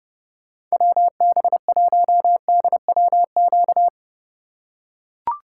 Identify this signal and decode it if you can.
Morse code, heard as a keyed tone, about 32 wpm, 700 Hz. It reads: WB1DWQ
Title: USA Call Signs - 6 Characters - 3x Character Spacing - 30wpm